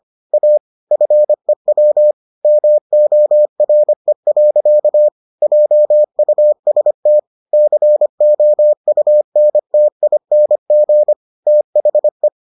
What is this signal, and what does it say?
Morse code, heard as a keyed tone, about 25 wpm, 600 Hz. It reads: A FEW MORE. JUST COUNTING THE